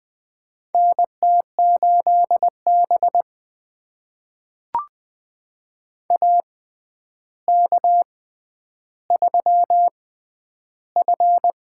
Morse code, heard as a keyed tone, about 20 wpm, spaced as for 10 wpm, 700 Hz. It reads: NT8B AK3F
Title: USA Call Signs - 6x Character Spacing - 20wpm